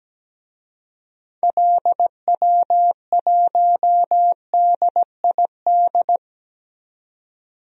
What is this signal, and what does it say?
Morse code, heard as a keyed tone, about 17 wpm, 700 Hz. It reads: LW1DID
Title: DXCC-100 Call Signs - 6 Characters - 17wpm